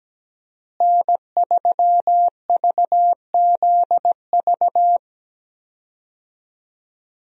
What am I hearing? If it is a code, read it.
Morse code, heard as a keyed tone, about 17 wpm, 700 Hz. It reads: N3VZV
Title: USA Call Signs - 5 Characters - Repeated 2x - 17wpm